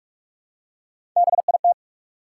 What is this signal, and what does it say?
Morse code, heard as a keyed tone, about 47 wpm, 700 Hz. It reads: BIT
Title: Three Letter Words - 45wpm